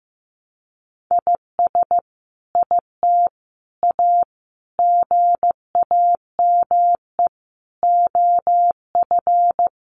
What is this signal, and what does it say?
Morse code, heard as a keyed tone, about 15 wpm, 700 Hz. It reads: IS IT A GAME OF
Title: Sentences from Top 1200 Words 15wpm